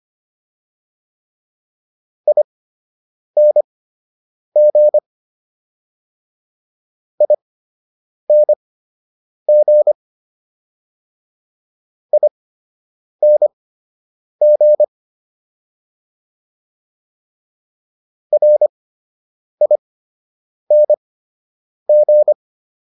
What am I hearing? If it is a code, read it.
Morse code, heard as a keyed tone, about 25 wpm, spaced as for 8 wpm, 600 Hz. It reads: ING ING ING RING